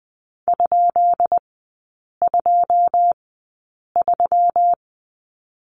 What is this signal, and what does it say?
Morse code, heard as a keyed tone, about 20 wpm, spaced as for 8 wpm, 700 Hz. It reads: ?23
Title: 172 Lesson 28e - Number 9 - Three Characters 20wpm